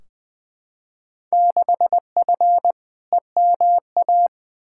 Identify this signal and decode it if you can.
Morse code, heard as a keyed tone, about 20 wpm, 700 Hz. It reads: 6F EMA
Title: Field Day Exchanges Encore - 20wpm